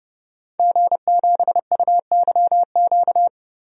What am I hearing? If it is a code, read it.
Morse code, heard as a keyed tone, about 30 wpm, 700 Hz. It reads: G7UYQ
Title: DXCC-100 Call Signs - 5 Characters - 30wpm